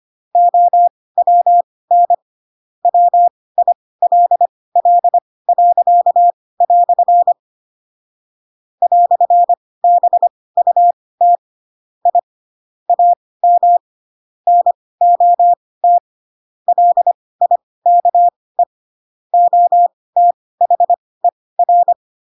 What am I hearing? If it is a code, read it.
Morse code, heard as a keyed tone, about 25 wpm, spaced as for 18 wpm, 700 Hz. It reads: OWN WILL." "BUT I AM NOT LIKE OTHER